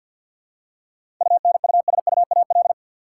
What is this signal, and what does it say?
Morse code, heard as a keyed tone, about 53 wpm, 700 Hz. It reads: UNUSUAL
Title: Seven Letter Words - 50wpm